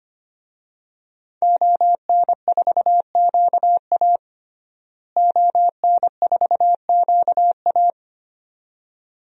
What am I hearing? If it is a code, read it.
Morse code, heard as a keyed tone, about 25 wpm, 700 Hz. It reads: ON4QA ON4QA
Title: DXCC-100 Call Signs - 5 Characters - Repeated 2x - 25wpm